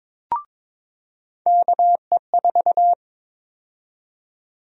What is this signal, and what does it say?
Morse code, heard as a keyed tone, about 22 wpm, 700 Hz. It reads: KE4
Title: USA Call Sign Prefixes - 22wpm